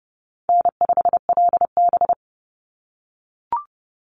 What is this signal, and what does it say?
Morse code, heard as a keyed tone, about 30 wpm, 700 Hz. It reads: N5LB
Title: Easy Call Signs - 30wpm